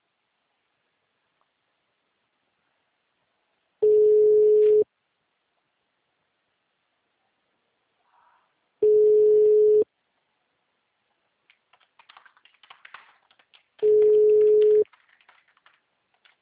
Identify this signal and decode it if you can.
Ringback tone (Europe, 425 Hz).